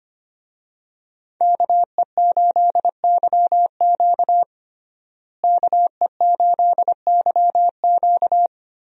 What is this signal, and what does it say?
Morse code, heard as a keyed tone, about 25 wpm, 700 Hz. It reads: KE8YQ KE8YQ
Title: USA Call Signs - 5 Characters - Repeated 2x - 25wpm